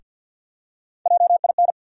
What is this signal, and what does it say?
Morse code, heard as a keyed tone, about 52 wpm, 700 Hz. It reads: JIN
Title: USA Call Sign Postfixes - 3 Letters 50wpm